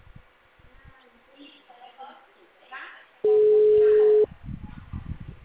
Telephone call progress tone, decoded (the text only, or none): ringback tone